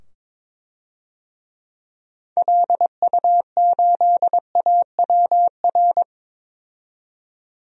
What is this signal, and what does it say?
Morse code, heard as a keyed tone, about 22 wpm, 700 Hz. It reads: LU8AWR
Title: DXCC-100 Call Signs - 6 Characters - 22wpm